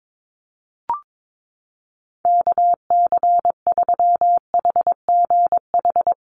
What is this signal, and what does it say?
Morse code, heard as a keyed tone, about 22 wpm, 700 Hz. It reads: KC3HGH
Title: USA Call Signs Repeated 2x 22wpm